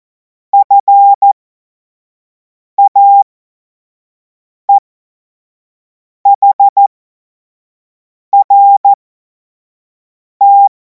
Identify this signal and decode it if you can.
Morse code, heard as a keyed tone, about 14 wpm, spaced as for 5 wpm, 800 Hz. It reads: FAEHRT
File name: Morsekurs / Witze / Witz 07. Juli 2026 25 F70